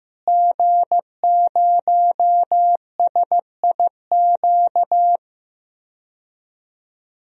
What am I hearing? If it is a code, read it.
Morse code, heard as a keyed tone, about 15 wpm, 700 Hz. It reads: G0SIQ
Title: DXCC-100 Call Signs - 5 Characters - Repeated 2x - 15wpm